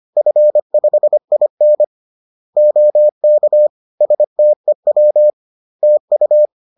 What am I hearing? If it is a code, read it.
Morse code, heard as a keyed tone, about 25 wpm, 600 Hz. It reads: F5IN OK STEW TU